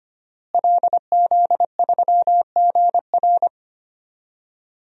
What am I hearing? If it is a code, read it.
Morse code, heard as a keyed tone, about 25 wpm, 700 Hz. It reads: LZ3GR
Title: DXCC-100 Call Signs - 5 Characters - Repeated 3x - 25wpm